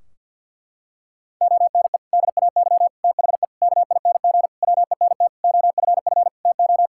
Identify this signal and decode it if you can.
Morse code, heard as a keyed tone, about 47 wpm, 700 Hz. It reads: ONE DAY THE KING WENT OFF TO
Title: Sentences from Top 1600 Words 50wpm